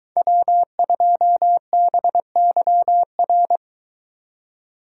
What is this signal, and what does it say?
Morse code, heard as a keyed tone, about 23 wpm, 700 Hz. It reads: W2BYR